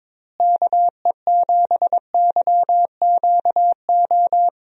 Morse code, heard as a keyed tone, about 23 wpm, 700 Hz. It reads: KE7YQO
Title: USA Call Signs - 6 Characters - Repeated 2x - 22wpm